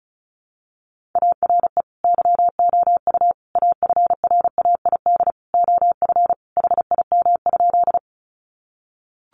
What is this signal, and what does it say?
Morse code, heard as a keyed tone, about 34 wpm, 700 Hz. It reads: ARE YOU AFRAID OF HIM?
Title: Sentences from Top 1500 Words 35wpm